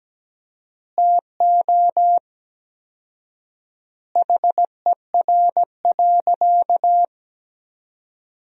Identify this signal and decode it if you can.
Morse code, heard as a keyed tone, about 17 wpm, 700 Hz. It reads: TO HER.
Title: 4x Word-Spacing - Sentences from Top 300 Words - 17wpm